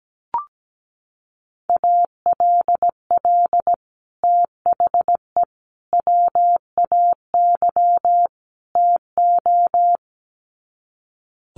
Morse code, heard as a keyed tone, about 17 wpm, 700 Hz. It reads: ALL THE WAY TO
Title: Sets of 4 Words 17wpm